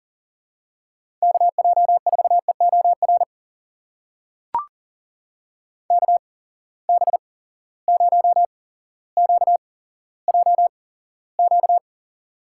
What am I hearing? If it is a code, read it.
Morse code, heard as a keyed tone, about 40 wpm, spaced as for 17 wpm, 700 Hz. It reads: KJ4EOR KB0QJQ
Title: USA Call Signs - 6 Characters - 8x Character Spacing - 40wpm